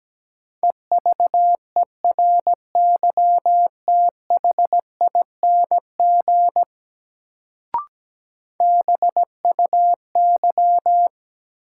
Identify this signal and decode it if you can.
Morse code, heard as a keyed tone, about 17 wpm, 700 Hz. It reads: EVERYTHING BUY
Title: Sentences from Top 700 Words - New Word Drill - 17wpm